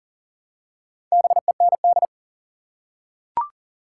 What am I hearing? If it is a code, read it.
Morse code, heard as a keyed tone, about 39 wpm, 700 Hz. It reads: BEND
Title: Four Letter Words - 40wpm